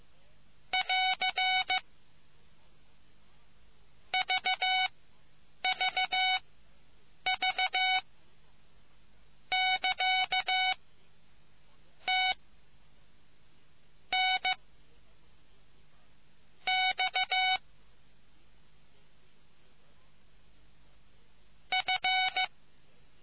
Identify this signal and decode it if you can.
Morse code, heard as a keyed tone, about 15 wpm, spaced as for 8 wpm, 750 Hz. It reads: <AR> VVV <KA> T N X F